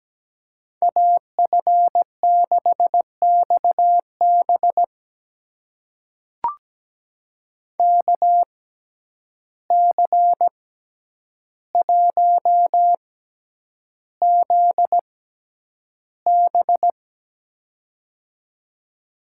Morse code, heard as a keyed tone, about 17 wpm, spaced as for 9 wpm, 700 Hz. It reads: AF6XB KC1ZB
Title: USA Call Signs - 5 Characters - 6x Character Spacing - 17wpm